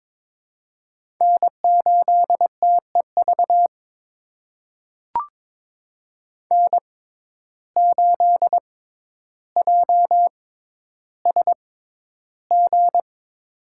Morse code, heard as a keyed tone, about 22 wpm, spaced as for 11 wpm, 700 Hz. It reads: N8TEV N8JSG
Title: USA Call Signs - 5 Characters - 6x Character Spacing - 22wpm